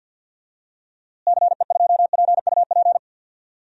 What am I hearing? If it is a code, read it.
Morse code, heard as a keyed tone, about 52 wpm, 700 Hz. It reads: CE2PUP
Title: DXCC-100 Call Signs - 6 Characters - Repeated 3x - 50wpm